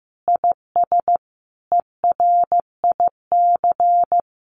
Morse code, heard as a keyed tone, about 15 wpm, 700 Hz. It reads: IS ERIC
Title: Common QSO Phrases 15wpm